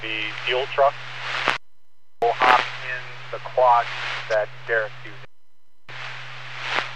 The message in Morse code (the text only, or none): none